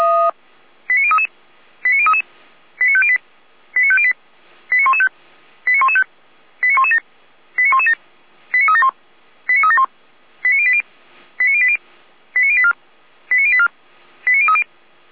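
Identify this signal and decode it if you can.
Five-tone selective call (ZVEI), heard as quick runs of five tones, tones of about 70 ms. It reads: 89033 89033 89508 89508 89115 89115 89117 89117 89371 89371 89080 89080 89084 89084 89030
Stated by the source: ZVEI five-tone selective call received by SDR.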